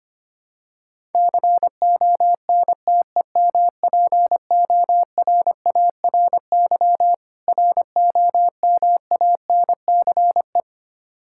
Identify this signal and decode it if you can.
Morse code, heard as a keyed tone, about 25 wpm, 700 Hz. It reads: CONTEMPORARY ROMANCE
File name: practice / 25wpm / Difficult Words with Hints - 25wpm